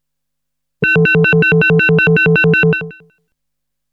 There is no code.